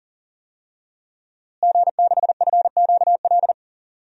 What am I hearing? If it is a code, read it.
Morse code, heard as a keyed tone, about 40 wpm, 700 Hz. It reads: G6FQL
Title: DXCC-100 Call Signs - 5 Characters - Repeated 2x - 40wpm